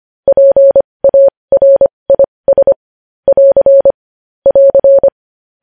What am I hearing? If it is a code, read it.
Morse code, heard as a keyed tone, about 25 wpm, spaced as for 20 wpm, 550 Hz. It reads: PARIS <AR> <AR>
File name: podcast / News headlines-1x-25x20